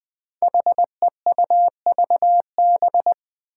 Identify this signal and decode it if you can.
Morse code, heard as a keyed tone, about 20 wpm, 700 Hz. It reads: HEUVB
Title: Five Challenging Letter-Number Combinations 20wpm